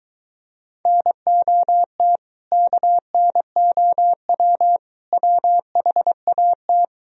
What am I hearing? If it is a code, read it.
Morse code, heard as a keyed tone, about 23 wpm, 700 Hz. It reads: NOT KNOW WHAT